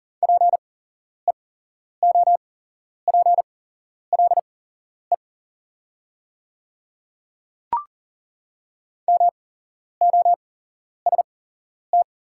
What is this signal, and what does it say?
Morse code, heard as a keyed tone, about 38 wpm, spaced as for 11 wpm, 700 Hz. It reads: PEOPLE MOST
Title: 8x Character Spacing - Top 500 Words - 40wpm